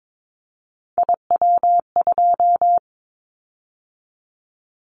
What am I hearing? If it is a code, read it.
Morse code, heard as a keyed tone, about 22 wpm, 700 Hz. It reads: IW2
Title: Callsign Prefixes for Top 25 DXCC - 22wpm